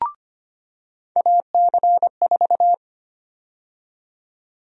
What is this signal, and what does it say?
Morse code, heard as a keyed tone, about 25 wpm, 700 Hz. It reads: AC4
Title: USA Call Sign Prefixes - 25wpm